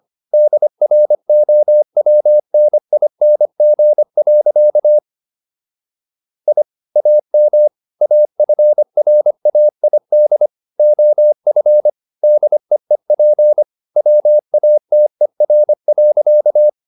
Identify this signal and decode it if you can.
Morse code, heard as a keyed tone, about 25 wpm, 600 Hz. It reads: DROWNING. I AM AFRAID OF DEEP WATER.